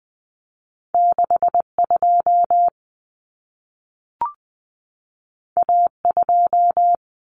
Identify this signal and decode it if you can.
Morse code, heard as a keyed tone, about 20 wpm, 700 Hz. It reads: 62 A2
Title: Common Cut Numbers - 2-Digit 20wpm